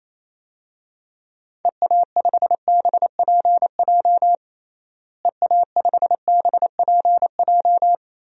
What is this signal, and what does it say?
Morse code, heard as a keyed tone, about 28 wpm, 700 Hz. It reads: EA5BPJ EA5BPJ
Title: DXCC-100 Call Signs - 6 Characters - Repeated 3x - 28wpm